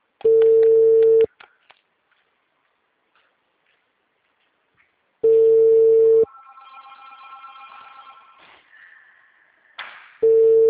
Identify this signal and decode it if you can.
Ringback tone (China, 450 Hz).
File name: out- -2048- - - .2855